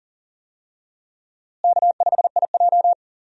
Morse code, heard as a keyed tone, about 37 wpm, 700 Hz. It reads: K5IJ
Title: USA Call Signs - 4 Characters - Repeated 2x - 40wpm